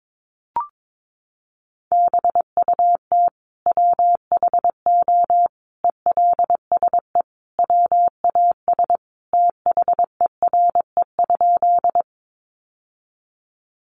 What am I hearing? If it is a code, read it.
Morse code, heard as a keyed tone, about 22 wpm, 700 Hz. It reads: BUT WHO ELSE WAS THERE?
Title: Sentences from Top 800 Words 22wpm